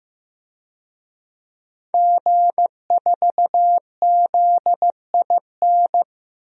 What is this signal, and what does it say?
Morse code, heard as a keyed tone, about 15 wpm, 700 Hz. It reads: G4ZIN
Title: DXCC-100 Call Signs - 5 Characters - Repeated 2x - 15wpm